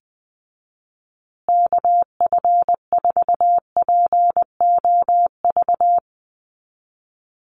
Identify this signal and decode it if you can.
Morse code, heard as a keyed tone, about 19 wpm, 700 Hz. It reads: KF4POV
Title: USA Call Signs Repeated 3x 20wpm